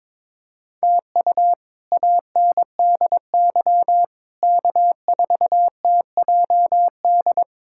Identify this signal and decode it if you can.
Morse code, heard as a keyed tone, about 22 wpm, 700 Hz. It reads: TU ANDY K4TJD
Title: K1USN SST QSO Practice - 22wpm